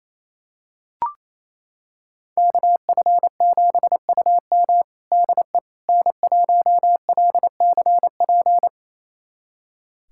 Morse code, heard as a keyed tone, about 28 wpm, 700 Hz. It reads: KF7UM DE N1LCP
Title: USA Callsign DE Callsign - 5 Characters - 28wpm